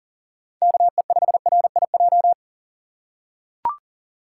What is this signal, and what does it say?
Morse code, heard as a keyed tone, about 40 wpm, 700 Hz. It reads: KE5RIJ
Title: USA Call Signs - 6 Characters - 4x Character Spacing - 40wpm